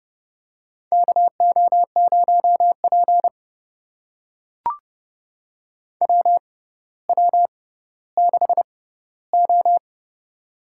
Morse code, heard as a keyed tone, about 30 wpm, spaced as for 15 wpm, 700 Hz. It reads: KO0P WW6O
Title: USA Call Signs - 4 Characters - 6x Character Spacing - 30wpm